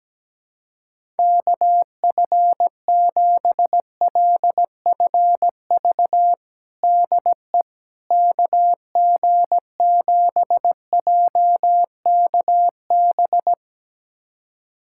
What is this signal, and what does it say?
Morse code, heard as a keyed tone, about 17 wpm, 700 Hz. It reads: KF7LFV DE KG7JKB
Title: USA Callsign DE Callsign 17wpm